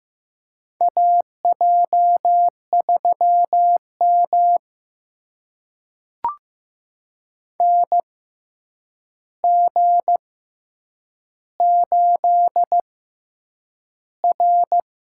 Morse code, heard as a keyed tone, about 15 wpm, spaced as for 8 wpm, 700 Hz. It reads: AJ3M NG8R
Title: USA Call Signs - 6x Character Spacing - 15wpm